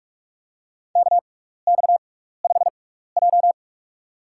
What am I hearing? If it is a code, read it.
Morse code, heard as a keyed tone, about 46 wpm, spaced as for 15 wpm, 700 Hz. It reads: KX5J
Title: USA Call Signs - 6x Character Spacing - 45wpm